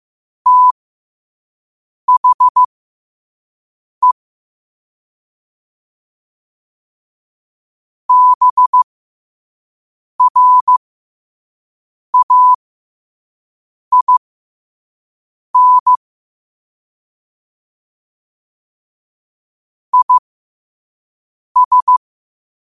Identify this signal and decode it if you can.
Morse code, heard as a keyed tone, about 15 wpm, spaced as for 5 wpm, 1000 Hz. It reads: THE BRAIN IS
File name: qotdfiles / 5wpm